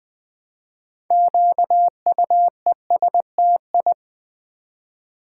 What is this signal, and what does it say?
Morse code, heard as a keyed tone, about 20 wpm, 700 Hz. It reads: QUESTI
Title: Build A Word - 2 Letters At A Time - 8-Letter Words - 20wpm